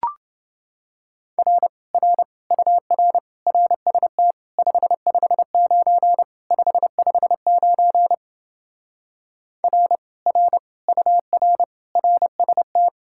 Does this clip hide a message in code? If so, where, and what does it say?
Yes: Morse code from 1.4 to 12.9 s, heard as a keyed tone, about 28 wpm, 700 Hz. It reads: R R UR RST 559 559 R R UR RST